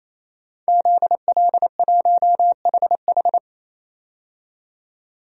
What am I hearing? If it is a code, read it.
Morse code, heard as a keyed tone, about 28 wpm, 700 Hz. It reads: ZL1HH
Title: DXCC-100 Call Signs - 5 Characters - 28wpm